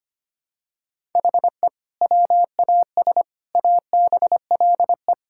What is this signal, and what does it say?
Morse code, heard as a keyed tone, about 25 wpm, 700 Hz. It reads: HE WAS ABLE
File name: practice / 25wpm / Sentences with 3-Word Phrases - 25wpm